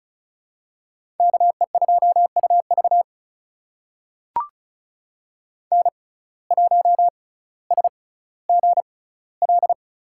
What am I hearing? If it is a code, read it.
Morse code, heard as a keyed tone, about 36 wpm, spaced as for 18 wpm, 700 Hz. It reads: KE2UV N1SGL
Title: USA Call Signs - 5 Characters - 6x Character Spacing - 35wpm